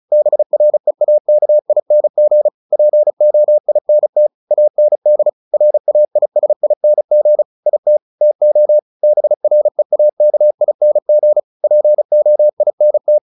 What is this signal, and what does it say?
Morse code, heard as a keyed tone, about 34 wpm, 600 Hz. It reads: BREAKING POINT AND RAISING IT TO BREAKING POINT
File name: the old man and the sea 2 35wpm